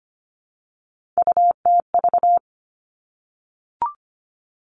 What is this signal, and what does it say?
Morse code, heard as a keyed tone, about 25 wpm, 700 Hz. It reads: UTV
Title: Three Challenging Letter-Number Combinations 25wpm